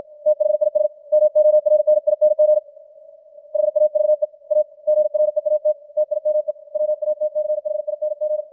Morse code, heard as a keyed tone, about 56 wpm, 600 Hz. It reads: THIS MORNING HAVE A GREAT TIME WATCHING
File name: archived / CWBros